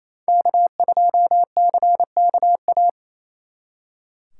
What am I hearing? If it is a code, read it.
Morse code, heard as a keyed tone, about 28 wpm, 700 Hz. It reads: K2CKA
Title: Speed Racing - USA Call Signs 28wpm